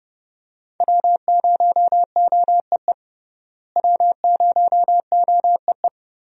Morse code, heard as a keyed tone, about 30 wpm, 700 Hz. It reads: W0OEE W0OEE
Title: USA Call Signs - 5 Characters - Repeated 3x - 30wpm